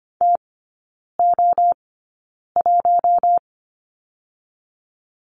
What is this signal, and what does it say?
Morse code, heard as a keyed tone, about 25 wpm, spaced as for 9 wpm, 700 Hz. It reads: TO1
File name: course / 25:10wpm / 093 Lesson 16e - Letter c - Three Characters 25wpm